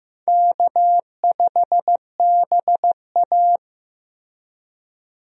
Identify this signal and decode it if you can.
Morse code, heard as a keyed tone, about 15 wpm, 700 Hz. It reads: K5BA
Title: USA Call Signs - 4 Characters - Repeated 2x - 15wpm